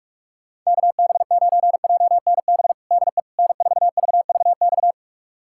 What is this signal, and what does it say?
Morse code, heard as a keyed tone, about 46 wpm, 700 Hz. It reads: KB9JNB DE N4VVX
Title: USA Callsign DE Callsign 45wpm